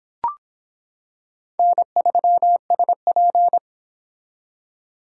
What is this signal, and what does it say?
Morse code, heard as a keyed tone, about 25 wpm, 700 Hz. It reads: N3SP